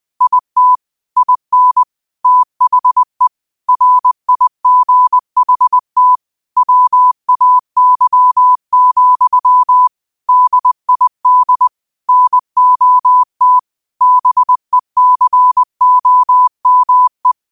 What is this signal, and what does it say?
Morse code, heard as a keyed tone, about 20 wpm, 1000 Hz. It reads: IT IN THE RIGHT WAY, DID NOT BECOME
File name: qotdfiles / 20wpm